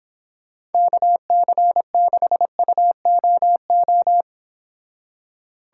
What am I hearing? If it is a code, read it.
Morse code, heard as a keyed tone, about 29 wpm, 700 Hz. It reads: KC6UOO